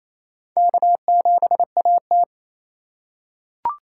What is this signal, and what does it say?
Morse code, heard as a keyed tone, about 28 wpm, 700 Hz. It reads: K7AT
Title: USA Call Signs - 4 Characters - 2x Character Spacing - 28wpm